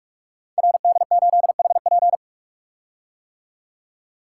Morse code, heard as a keyed tone, about 44 wpm, 700 Hz. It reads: RD8HP
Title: Code Groups - 5 Letter:Numbers - 45wpm